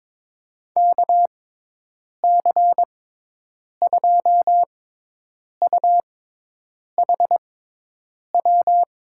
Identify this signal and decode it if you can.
Morse code, heard as a keyed tone, about 21 wpm, spaced as for 8 wpm, 700 Hz. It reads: KC2UHW
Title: USA Call Signs - 6 Characters - 6x Character Spacing - 22wpm